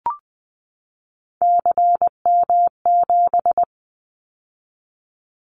Morse code, heard as a keyed tone, about 20 wpm, 700 Hz. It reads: CM7
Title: Callsign Prefixes for Top 100 DXCC - 20wpm